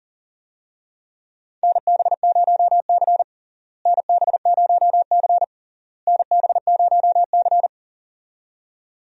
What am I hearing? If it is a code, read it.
Morse code, heard as a keyed tone, about 40 wpm, 700 Hz. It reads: NB0C NB0C NB0C
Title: USA Call Signs - 4 Characters - Repeated 3x - 40wpm